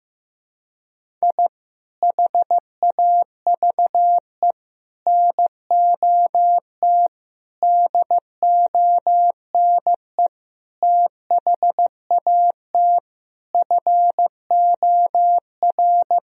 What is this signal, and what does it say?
Morse code, heard as a keyed tone, about 15 wpm, 700 Hz. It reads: I HAVE NOT DONE THAT FOR
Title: Sentences from Top 1000 Words - Review Part 1 - 15wpm